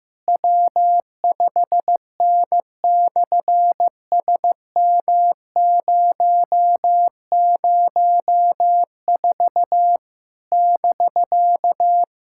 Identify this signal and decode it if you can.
Morse code, heard as a keyed tone, about 15 wpm, 700 Hz. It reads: W5N/SM004 <BK>
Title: SOTA QSO Practice 15wpm